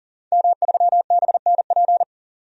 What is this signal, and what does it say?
Morse code, heard as a keyed tone, about 39 wpm, 700 Hz. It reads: M3BNP
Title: DXCC-100 Call Signs - 5 Characters - 40wpm